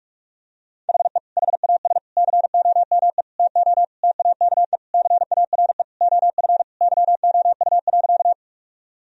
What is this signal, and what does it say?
Morse code, heard as a keyed tone, about 41 wpm, 700 Hz. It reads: HE HAS COME TO TAKE CARE OF YOU.
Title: Sentences from Top 1000 Words - Review Part 1 - 45wpm